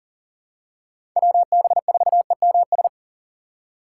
Morse code, heard as a keyed tone, about 40 wpm, 700 Hz. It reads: WB4EMS
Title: USA Call Signs - 3x Character Spacing - 40wpm